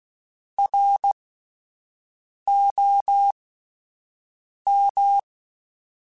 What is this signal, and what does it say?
Morse code, heard as a keyed tone, about 16 wpm, spaced as for 5 wpm, 800 Hz. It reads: ROM
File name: Spaced Characters / Spaced Characters-mp3s / Weather 5 1 sp @10